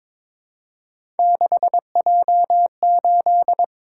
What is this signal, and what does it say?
Morse code, heard as a keyed tone, about 23 wpm, 700 Hz. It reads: 6J8
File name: practice / 22wpm / Three Letter-Number Combinations - 22wpm